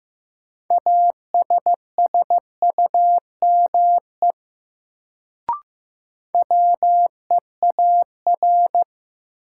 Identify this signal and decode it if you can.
Morse code, heard as a keyed tone, about 15 wpm, 700 Hz. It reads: ASSUME WEAR
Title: Sentences from Top 1500 Words - New Word Drill Alt - 15wpm